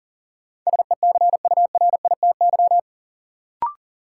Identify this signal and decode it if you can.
Morse code, heard as a keyed tone, about 40 wpm, 700 Hz. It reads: SECURITY
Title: Sentences from Top 1400 Words - New Word Drill Alt - 40wpm